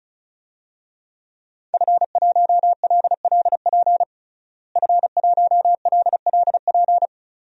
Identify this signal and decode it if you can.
Morse code, heard as a keyed tone, about 36 wpm, 700 Hz. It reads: F1LLP F1LLP
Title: DXCC-100 Call Signs - 5 Characters - Repeated 2x - 35wpm